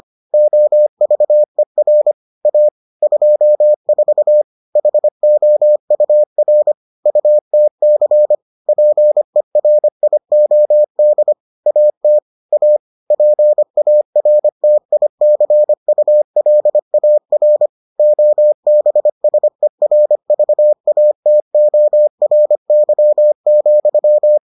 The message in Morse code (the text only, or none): OVER A 24 HOUR UTC PERIOD AT A PARTICULAR OBSERVATORY,